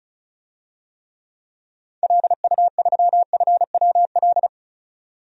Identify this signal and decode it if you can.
Morse code, heard as a keyed tone, about 34 wpm, 700 Hz. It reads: LU3FWL
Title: DXCC-100 Call Signs - 6 Characters - 35wpm